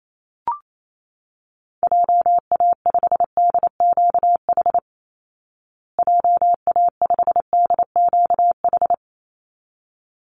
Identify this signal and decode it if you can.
Morse code, heard as a keyed tone, about 28 wpm, 690 Hz. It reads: JA5DQH JA5DQH
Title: Call Signs Repeated 3x - 28wpm